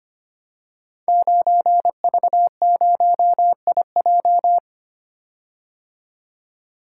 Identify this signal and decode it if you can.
Morse code, heard as a keyed tone, about 25 wpm, 700 Hz. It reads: 9V0IJ
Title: Code Groups - 5 Letter:Numbers - 25wpm